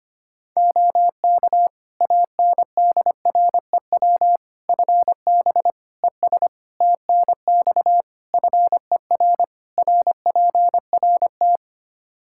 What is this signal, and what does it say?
Morse code, heard as a keyed tone, about 25 wpm, 700 Hz. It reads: OK ANDREW FB ES TNX FER RPRT
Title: Practice QSO - Exchange 2 - Info - 25wpm